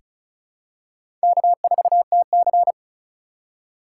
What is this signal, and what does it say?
Morse code, heard as a keyed tone, about 36 wpm, 700 Hz. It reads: K4TC
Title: USA Call Signs - 4 Characters - 4x Character Spacing - 35wpm